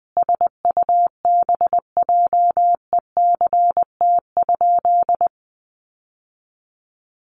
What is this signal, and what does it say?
Morse code, heard as a keyed tone, about 20 wpm, 700 Hz. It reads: SUBJECT?
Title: Sentences from Top 800 Words 20wpm